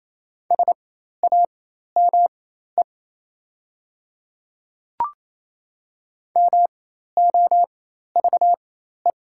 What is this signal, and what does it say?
Morse code, heard as a keyed tone, about 28 wpm, spaced as for 13 wpm, 700 Hz. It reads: SAME MOVE
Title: 4x Character Spacing - Top 200 Words - 28wpm